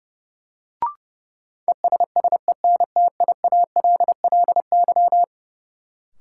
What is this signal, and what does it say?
Morse code, heard as a keyed tone, about 32 wpm, 700 Hz. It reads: ESSENTIALLY
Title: Sentences from Top 1300 Words - New Word Drill - 30wpm